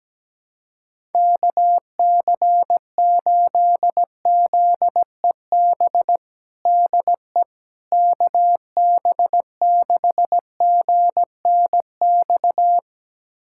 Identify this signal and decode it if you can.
Morse code, heard as a keyed tone, about 17 wpm, 700 Hz. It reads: KC8ZEB DE KB6GNX
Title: USA Callsign DE Callsign 17wpm